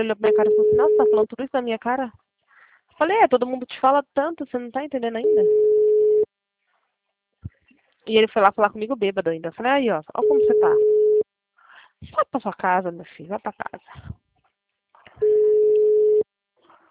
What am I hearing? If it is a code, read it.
Ringback tone (Europe, 425 Hz), with speech.
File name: out- -2010- - - .513